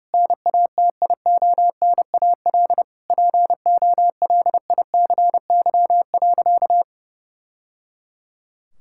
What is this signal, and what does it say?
Morse code, heard as a keyed tone, about 30 wpm, 700 Hz. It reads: NATIONAL POLICY.
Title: Sentences from Top 1000 Words - Review Part 6 - 30wpm